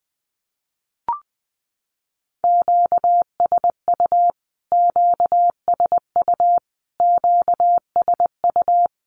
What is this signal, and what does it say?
Morse code, heard as a keyed tone, about 20 wpm, 700 Hz. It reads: QSU QSU QSU
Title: Q-Codes Repeated 3x 20wpm